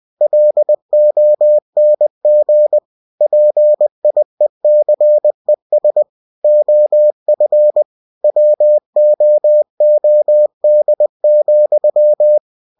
Morse code, heard as a keyed tone, about 20 wpm, 600 Hz. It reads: LONG PIECES OF WOOD,
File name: cw-books / winnie-the-pooh / winnie-20wpm-page-0106